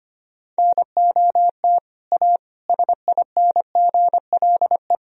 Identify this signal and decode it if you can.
Morse code, heard as a keyed tone, about 25 wpm, 700 Hz. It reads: NOT A SINGLE
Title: Sentences from Top 1000 Words - Review Part 4 - 25wpm